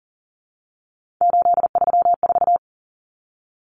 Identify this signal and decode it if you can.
Morse code, heard as a keyed tone, about 40 wpm, 700 Hz. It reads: 834
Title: Common Cut Numbers - 3-Digit 40wpm